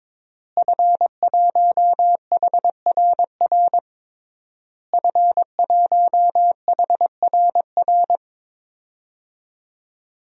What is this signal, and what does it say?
Morse code, heard as a keyed tone, about 22 wpm, 700 Hz. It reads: F1HRR F1HRR
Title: DXCC-100 Call Signs - 5 Characters - Repeated 3x - 22wpm